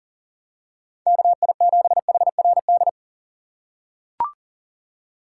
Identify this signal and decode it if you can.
Morse code, heard as a keyed tone, about 40 wpm, 700 Hz. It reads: KI7HRD
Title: USA Call Signs - 2x Character Spacing - 40wpm